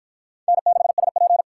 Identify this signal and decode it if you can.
Morse code, heard as a keyed tone, about 53 wpm, 700 Hz. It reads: N6SP